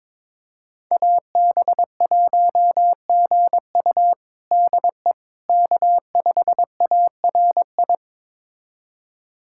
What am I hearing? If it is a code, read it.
Morse code, heard as a keyed tone, about 22 wpm, 700 Hz. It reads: AB1GU DE K5ARI
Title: USA Callsign DE Callsign - 5 Characters - 22wpm